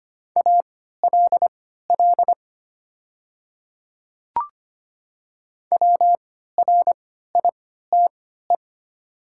Morse code, heard as a keyed tone, about 25 wpm, spaced as for 14 wpm, 700 Hz. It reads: ALL WRITE
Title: 3x Character Spacing - Top 200 Words - 25wpm